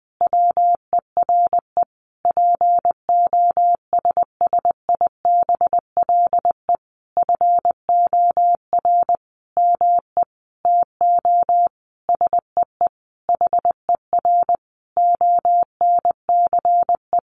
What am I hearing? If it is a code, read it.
Morse code, heard as a keyed tone, about 20 wpm, 700 Hz. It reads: WERE POSSIBLE FOR ME TO SEE HER ONCE
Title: Sentences from Top 500 Words 20wpm